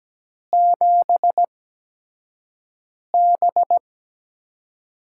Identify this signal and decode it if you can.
Morse code, heard as a keyed tone, about 17 wpm, spaced as for 5 wpm, 700 Hz. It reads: 7B
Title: USA Call Signs - 6 Characters - 8x Character Spacing - 17wpm